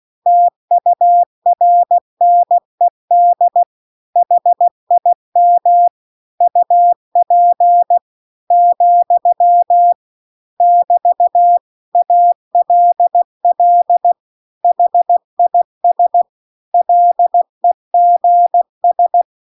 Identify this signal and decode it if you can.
Morse code, heard as a keyed tone, about 16 wpm, 700 Hz. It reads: TURNED HIM UP , <BT> ALL HIS LEGS